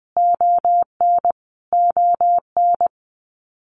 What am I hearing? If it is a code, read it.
Morse code, heard as a keyed tone, about 20 wpm, 700 Hz. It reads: ON ON
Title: Canadian Provinces and Territories - Repeated 3x - 20wpm